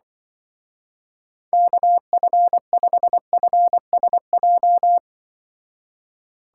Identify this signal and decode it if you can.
Morse code, heard as a keyed tone, about 25 wpm, 700 Hz. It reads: KF5FSJ
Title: Speed Racing - USA 2x3 Call Signs 20wpm